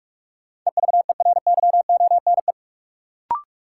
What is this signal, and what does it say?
Morse code, heard as a keyed tone, about 46 wpm, 700 Hz. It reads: EVERYONE
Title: Sentences from Top 800 Words - New Word Drill - 45wpm